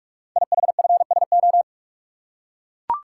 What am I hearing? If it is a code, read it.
Morse code, heard as a keyed tone, about 44 wpm, 700 Hz. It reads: IHFSO
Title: Code Groups - 5 Letters - 45wpm